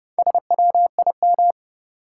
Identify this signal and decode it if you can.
Morse code, heard as a keyed tone, about 30 wpm, 700 Hz. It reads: SWIM
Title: Sentences from Top 1000 Words - Review Part 6 - New Word Drill - 30wpm